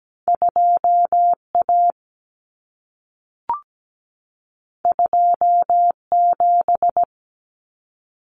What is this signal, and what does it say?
Morse code, heard as a keyed tone, about 17 wpm, 700 Hz. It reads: 2A 27
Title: Common Cut Numbers - 2-Digit 17wpm